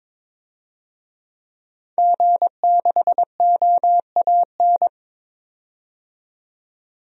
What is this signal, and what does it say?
Morse code, heard as a keyed tone, about 22 wpm, 700 Hz. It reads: G6OAN
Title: DXCC-100 Call Signs - 5 Characters - 22wpm